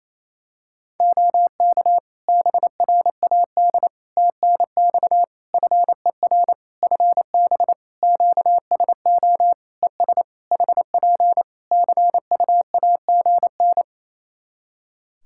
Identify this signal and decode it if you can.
Morse code, heard as a keyed tone, about 27 wpm, 700 Hz. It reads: OK BRAD TNX FER FB QSO ES HP CUAGN
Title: K1USN SST QSO Practice 28wpm